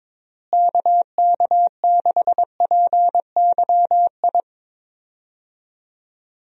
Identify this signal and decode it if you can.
Morse code, heard as a keyed tone, about 21 wpm, 700 Hz. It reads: KK6PYI
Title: USA Call Signs - 6 Characters - Repeated 3x - 22wpm